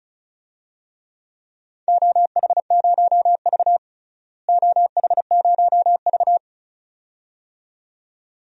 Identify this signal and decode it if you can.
Morse code, heard as a keyed tone, about 34 wpm, 700 Hz. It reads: OH0V OH0V
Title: DXCC-100 Call Signs - 4 Characters - Repeated 2x - 35wpm